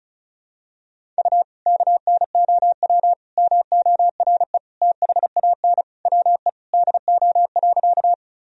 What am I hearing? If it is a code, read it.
Morse code, heard as a keyed tone, about 34 wpm, 700 Hz. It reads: U KNOW MORE THAN WE DO.
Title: Sentences from Top 100 Words and Abbreviations 35wpm